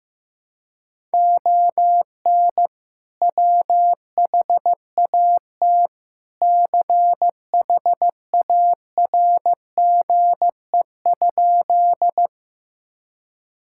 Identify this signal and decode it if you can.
Morse code, heard as a keyed tone, about 15 wpm, 700 Hz. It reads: ON WHAT CHARGE?
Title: Sentences from Top 1000 Words - Review Part 6 - 15wpm